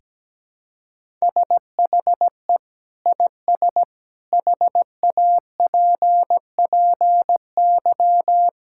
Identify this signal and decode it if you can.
Morse code, heard as a keyed tone, about 17 wpm, 700 Hz. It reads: SHE IS HAPPY
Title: Sentences from Top 1000 Words - Review Part 6 - 17wpm